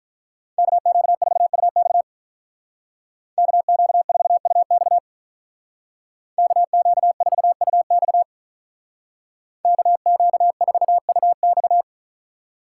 Morse code, heard as a keyed tone, about 45 wpm, 700 Hz. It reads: KQ4UX KQ4UX KQ4UX KQ4UX